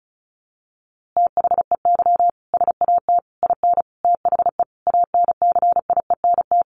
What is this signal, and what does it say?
Morse code, heard as a keyed tone, about 34 wpm, 700 Hz. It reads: THEY SAT IN THE ANCIENT
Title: A Sound Of Thunder 35wpm